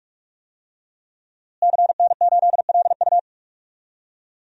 Morse code, heard as a keyed tone, about 43 wpm, 700 Hz. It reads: CN8LU
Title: DXCC-100 Call Signs - 5 Characters - Repeated 2x - 45wpm